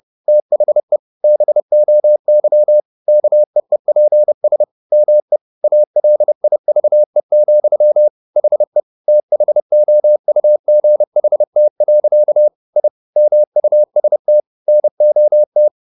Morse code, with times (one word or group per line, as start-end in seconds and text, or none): 0.3-1.0 THE
1.2-2.8 BOY
3.1-4.6 KEEPS
4.9-5.4 ME
5.6-8.1 ALIVE,
8.4-8.8 HE
9.1-12.5 THOUGHT.
12.8-12.9 I
13.2-14.4 MUST
14.7-15.7 NOT